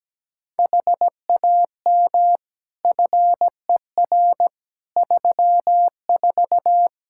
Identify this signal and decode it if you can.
Morse code, heard as a keyed tone, about 17 wpm, 700 Hz. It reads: HAM FER 34
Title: Practice QSO - Exchange 3 - More Info - 17wpm